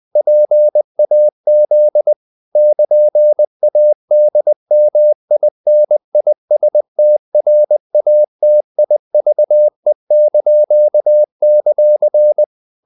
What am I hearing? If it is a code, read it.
Morse code, heard as a keyed tone, about 20 wpm, 600 Hz. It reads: PAZ <KN>ADMINISTRATIVE);
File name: CW / Practice / NationalCapitals20